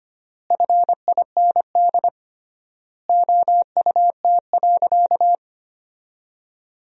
Morse code, heard as a keyed tone, about 25 wpm, 700 Hz. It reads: FIND OUT.
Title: 3x Word-Spacing - Sentences from Top 500 Words - 25wpm